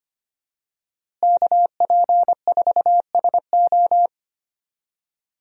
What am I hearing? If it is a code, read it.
Morse code, heard as a keyed tone, about 25 wpm, 700 Hz. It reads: KP4SO
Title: DXCC-100 Call Signs - 5 Characters - Repeated 2x - 25wpm